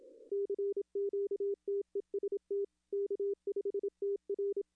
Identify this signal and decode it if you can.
Morse code, heard as a keyed tone, about 27 wpm, 400 Hz. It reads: CQTEST K5TR
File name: contests / arrl-160-k5tr-op-w5tn -1300